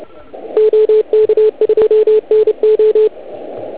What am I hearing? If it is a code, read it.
Morse code, heard as a keyed tone, about 30 wpm, 430 Hz, with radio noise. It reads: OK2NO